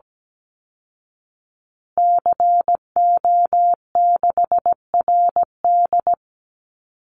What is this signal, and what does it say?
Morse code, heard as a keyed tone, about 17 wpm, 700 Hz. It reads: CO6RD
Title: Call Signs Repeated 2x - 17wpm